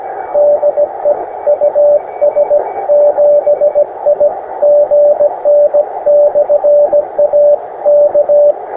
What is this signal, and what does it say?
Morse code, heard as a keyed tone, about 17 wpm, 590 Hz. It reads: DE US7IGN/A K
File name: attachments / europe 20m